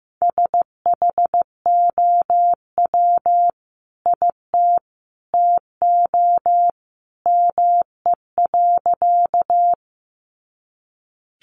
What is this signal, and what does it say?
Morse code, heard as a keyed tone, about 15 wpm, 700 Hz. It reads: SHOW IT TO ME.
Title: Sentences from Top 200 Words 15wpm